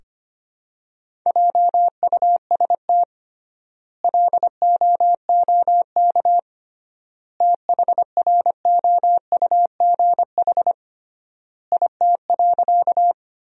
Morse code, heard as a keyed tone, about 25 wpm, 700 Hz. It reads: JUST LOOK THROUGH IT.
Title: 3x Word-Spacing - Sentences from Top 200 Words - 25wpm